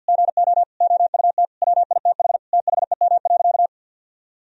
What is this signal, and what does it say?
Morse code, heard as a keyed tone, about 50 wpm, 700 Hz. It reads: GO OUT WITH THEM.
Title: Sentences from Top 1000 Words - Review Part 3 - 50wpm